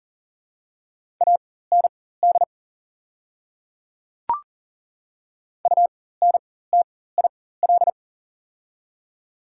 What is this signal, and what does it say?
Morse code, heard as a keyed tone, about 40 wpm, spaced as for 19 wpm, 690 Hz. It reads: AND UNTIL
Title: 4x Character Spacing - Top 500 Words - 40wpm